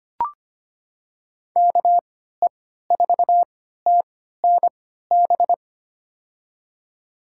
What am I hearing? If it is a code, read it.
Morse code, heard as a keyed tone, about 25 wpm, spaced as for 14 wpm, 700 Hz. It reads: KE4TNB
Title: USA Call Signs - 6 Characters - 3x Character Spacing - 25wpm